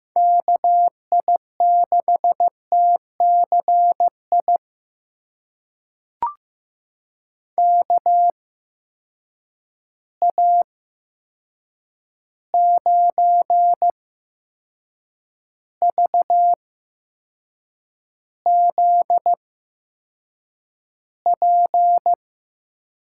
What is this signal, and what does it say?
Morse code, heard as a keyed tone, about 15 wpm, spaced as for 6 wpm, 700 Hz. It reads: KI6TCI KA9VZP
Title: USA Call Signs - 6 Characters - 8x Character Spacing - 15wpm